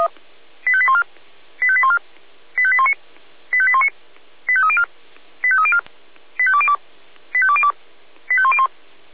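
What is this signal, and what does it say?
Five-tone selective call (ZVEI), heard as quick runs of five tones, tones of about 70 ms. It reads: 85814 85814 85819 85819 85304 85304 85202 85202 85101
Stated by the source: ZVEI five-tone alert received by SDR.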